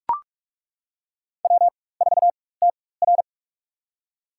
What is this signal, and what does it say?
Morse code, heard as a keyed tone, about 42 wpm, spaced as for 21 wpm, 700 Hz. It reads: W4TR
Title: USA Call Signs - 4 Characters - 4x Character Spacing - 45wpm